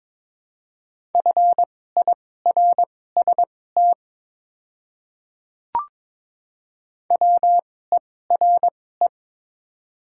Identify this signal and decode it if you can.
Morse code, heard as a keyed tone, about 21 wpm, spaced as for 16 wpm, 700 Hz. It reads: FIRST WERE
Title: 2x Character Spacing - Top 100 Words - 22wpm